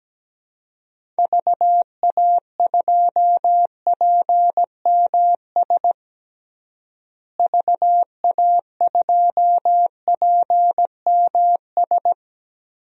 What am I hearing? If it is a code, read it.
Morse code, heard as a keyed tone, about 17 wpm, 700 Hz. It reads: VA2PMS VA2PMS
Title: DXCC-100 Call Signs - 6 Characters - Repeated 2x - 17wpm